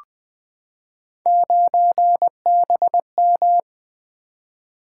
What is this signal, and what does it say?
Morse code, heard as a keyed tone, about 20 wpm, 700 Hz. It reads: 9BM
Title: Three Letter-Number Combinations - 20wpm